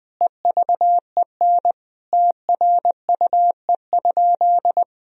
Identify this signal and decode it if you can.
Morse code, heard as a keyed tone, about 20 wpm, 700 Hz. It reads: EVEN TRUE?
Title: Sentences from Top 1000 Words - Review Part 2 - 20wpm